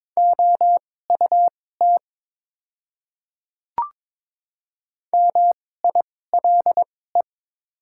Morse code, heard as a keyed tone, about 22 wpm, spaced as for 16 wpm, 700 Hz. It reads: OUT MILE
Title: 2x Character Spacing - Top 500 Words - 22wpm